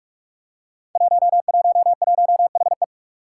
Morse code, heard as a keyed tone, about 44 wpm, 700 Hz. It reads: 111HE
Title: Five Challenging Letter-Number Combinations 45wpm